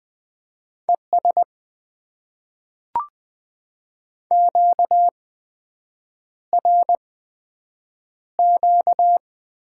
Morse code, heard as a keyed tone, about 20 wpm, spaced as for 7 wpm, 700 Hz. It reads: ES QRQ
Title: Top 90 QSO Elements - 8x Character Spacing - 20wpm